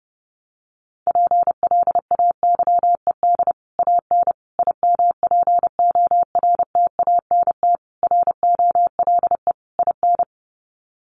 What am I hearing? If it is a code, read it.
Morse code, heard as a keyed tone, about 30 wpm, 700 Hz. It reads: PLAYED AN IMPORTANT ROLE IN
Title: Sets of 5 Words Encore2 30wpm